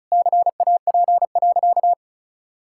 Morse code, heard as a keyed tone, about 36 wpm, 700 Hz. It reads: CAP.
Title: Sentences from Top 1600 Words 35wpm